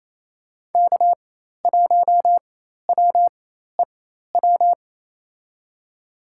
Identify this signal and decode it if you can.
Morse code, heard as a keyed tone, about 28 wpm, spaced as for 13 wpm, 700 Hz. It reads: K1WEW
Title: USA Call Signs - 4x Character Spacing - 28wpm